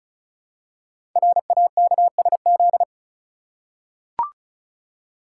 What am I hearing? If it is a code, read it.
Morse code, heard as a keyed tone, about 36 wpm, 700 Hz. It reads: RAKSZ
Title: Code Groups - 5 Letter:Numbers - 35wpm